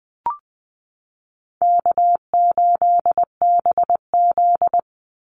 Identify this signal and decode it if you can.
Morse code, heard as a keyed tone, about 20 wpm, 700 Hz. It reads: K8BZ
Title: Easy Call Signs Repeated 2x - 20wpm